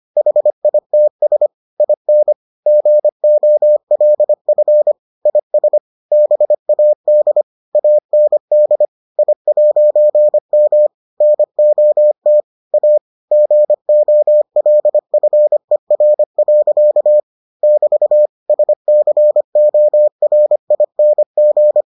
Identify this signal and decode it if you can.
Morse code, heard as a keyed tone, about 25 wpm, 600 Hz. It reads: HITS IN GOLF IS BAD AND I'M NOT A GOLFER. <BT> SCORING